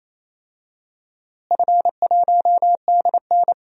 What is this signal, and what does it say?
Morse code, heard as a keyed tone, about 27 wpm, 700 Hz. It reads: F1DN
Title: DXCC-100 Call Signs - 4 Characters - Repeated 3x - 28wpm